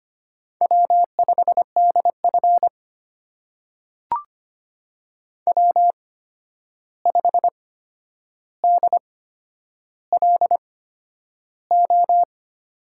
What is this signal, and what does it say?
Morse code, heard as a keyed tone, about 25 wpm, spaced as for 10 wpm, 700 Hz. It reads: W5DF W5DLO
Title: USA Call Signs - 8x Character Spacing - 25wpm